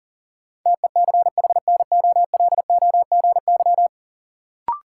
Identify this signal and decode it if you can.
Morse code, heard as a keyed tone, about 37 wpm, 700 Hz. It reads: TECHNOLOGY
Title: Sentences from Top 1200 Words - New Word Drill Alt - 40wpm